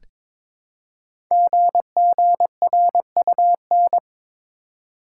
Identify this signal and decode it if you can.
Morse code, heard as a keyed tone, about 22 wpm, 700 Hz. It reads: GGRUN
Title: Code Groups - 5 Letters - 22wpm